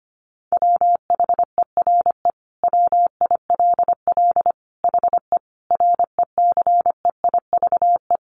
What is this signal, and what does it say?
Morse code, heard as a keyed tone, about 25 wpm, 700 Hz. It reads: WHERE WILL HE RECEIVE
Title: Sentences from Top 1000 Words 25wpm